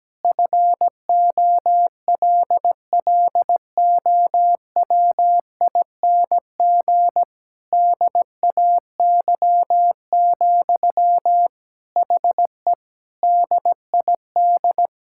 Morse code, heard as a keyed tone, about 17 wpm, 700 Hz. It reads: FOLLOWING DAY, HE DID
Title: Sentences from Top 1000 Words - Review Part 6 - 17wpm